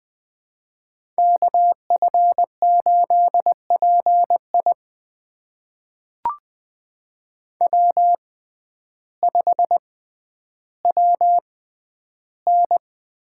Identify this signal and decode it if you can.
Morse code, heard as a keyed tone, about 20 wpm, spaced as for 11 wpm, 700 Hz. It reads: KF8PI W5WN
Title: USA Call Signs - 5 Characters - 6x Character Spacing - 20wpm